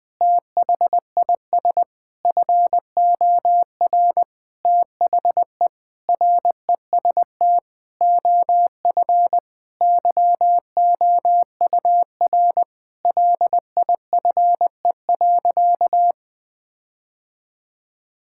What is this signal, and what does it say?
Morse code, heard as a keyed tone, about 20 wpm, 700 Hz. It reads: THIS FOR THE REST OF YOUR LIFE.
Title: Sentences with 5-Word Phrases - 20wpm